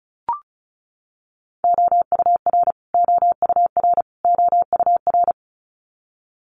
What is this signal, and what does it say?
Morse code, heard as a keyed tone, about 34 wpm, 700 Hz. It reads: OUR OUR OUR
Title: Top 200 Words Repeated 35wpm